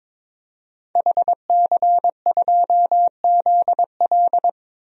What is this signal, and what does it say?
Morse code, heard as a keyed tone, about 22 wpm, 700 Hz. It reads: HC2ZL
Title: DXCC-100 Call Signs - 5 Characters - Repeated 2x - 22wpm